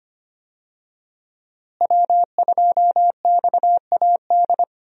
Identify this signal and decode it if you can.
Morse code, heard as a keyed tone, about 25 wpm, 700 Hz. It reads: W2XAD
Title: USA Call Signs - 5 Characters - Repeated 3x - 25wpm